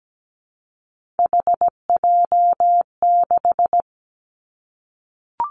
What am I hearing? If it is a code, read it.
Morse code, heard as a keyed tone, about 17 wpm, 700 Hz. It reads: HJ6
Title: Callsign Prefixes for Top 100 DXCC - 17wpm